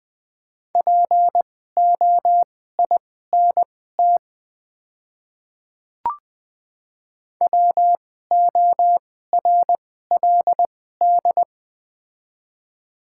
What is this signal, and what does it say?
Morse code, heard as a keyed tone, about 21 wpm, spaced as for 15 wpm, 690 Hz. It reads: POINT WORLD
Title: 2x Character Spacing - Top 200 Words - 20wpm